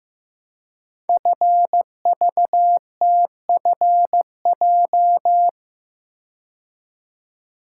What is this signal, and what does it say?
Morse code, heard as a keyed tone, about 15 wpm, 700 Hz. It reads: FVTFJ
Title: Code Groups - 5 Letters - 15wpm